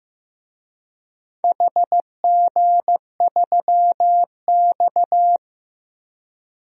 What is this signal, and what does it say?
Morse code, heard as a keyed tone, about 15 wpm, 700 Hz. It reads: HG3X
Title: DXCC-100 Call Signs - 4 Characters - Repeated 2x - 15wpm